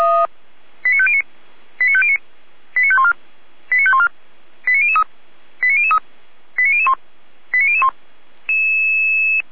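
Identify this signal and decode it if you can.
Five-tone selective call (ZVEI), heard as quick runs of five tones, tones of about 70 ms. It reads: 89509 89509 89514 89514 89003 89003 89001 89001
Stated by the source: ZVEI five-tone call received by SDR, archived on the Feuerwehr Orlamünde audio server (f85360).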